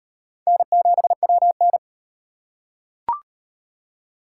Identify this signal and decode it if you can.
Morse code, heard as a keyed tone, about 35 wpm, 700 Hz. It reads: N7WN